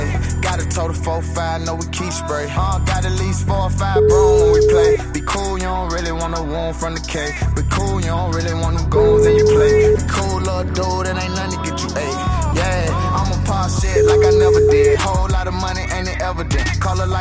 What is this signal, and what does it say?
Ringback tone (Europe, 425 Hz), with noise.